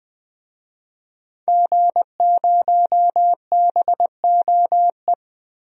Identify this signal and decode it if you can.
Morse code, heard as a keyed tone, about 21 wpm, 700 Hz. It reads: G0BOE
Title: DXCC-100 Call Signs - 5 Characters - Repeated 2x - 20wpm